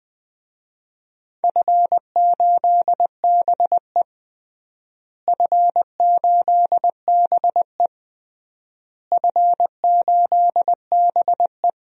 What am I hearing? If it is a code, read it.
Morse code, heard as a keyed tone, about 21 wpm, 700 Hz. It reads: F8BE F8BE F8BE
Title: DXCC-100 Call Signs - 4 Characters - Repeated 3x - 20wpm